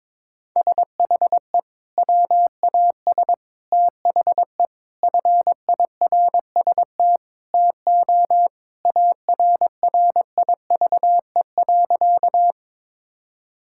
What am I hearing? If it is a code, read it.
Morse code, heard as a keyed tone, about 22 wpm, 700 Hz. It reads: SHE WAS THE FIRST TO ARRIVE.
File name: practice / 22wpm / Sentences with 3-Word Phrases - 22wpm